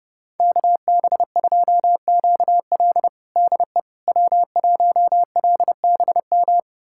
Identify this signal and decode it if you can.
Morse code, heard as a keyed tone, about 30 wpm, 700 Hz. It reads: KB2QL DE W1LBM
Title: USA Callsign DE Callsign - 5 Characters - 30wpm